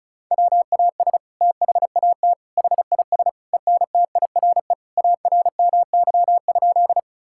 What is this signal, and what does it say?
Morse code, heard as a keyed tone, about 36 wpm, 690 Hz. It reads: WAS THAT HIS ENTIRE ARMY?
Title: Sentences from Top 1000 Words - Review Part 5 - 35wpm